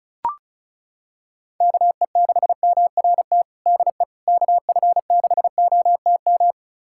Morse code, heard as a keyed tone, about 36 wpm, 700 Hz. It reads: KE6MRT DE KF6OTM
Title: USA Callsign DE Callsign - 6 Characters - 35wpm